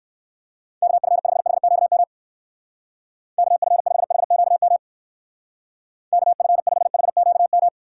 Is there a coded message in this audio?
Yes: Morse code from 0.8 to 7.7 s, heard as a keyed tone, about 59 wpm, 700 Hz. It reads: KF5HQM KF5HQM KF5HQM